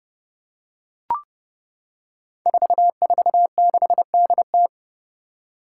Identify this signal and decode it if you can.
Morse code, heard as a keyed tone, about 32 wpm, 700 Hz. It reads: 446DT
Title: Five Challenging Letter-Number Combinations 30wpm